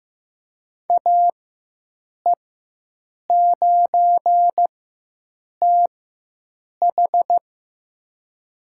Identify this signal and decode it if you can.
Morse code, heard as a keyed tone, about 15 wpm, spaced as for 7 wpm, 700 Hz. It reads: AE9TH
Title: USA Call Signs - 4x Character Spacing - 15wpm